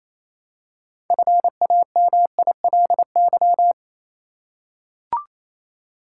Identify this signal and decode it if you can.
Morse code, heard as a keyed tone, about 28 wpm, 700 Hz. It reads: FAMILY
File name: Easy Words - Speed-Racing with Hints - 22wpm